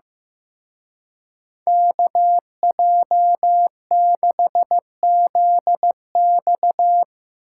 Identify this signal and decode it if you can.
Morse code, heard as a keyed tone, about 16 wpm, 700 Hz. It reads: KJ6ZX
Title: Speed Racing - USA 2x2 Call Signs 15wpm